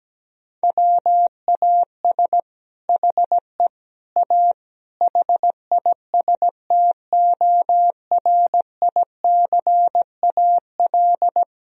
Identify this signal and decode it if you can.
Morse code, heard as a keyed tone, about 17 wpm, 700 Hz. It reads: WAS HE A HISTORICAL
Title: Sentences from Top 1000 Words - Review Part 4 - 17wpm